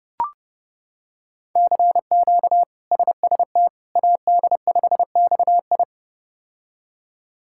Morse code, heard as a keyed tone, about 32 wpm, 700 Hz. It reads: CQ SST AD5XI
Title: K1USN SST QSO Practice - 30wpm